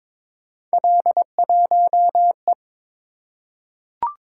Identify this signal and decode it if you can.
Morse code, heard as a keyed tone, about 22 wpm, 700 Hz. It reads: L1E
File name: practice / 22wpm / Three Letter-Number Combinations - 22wpm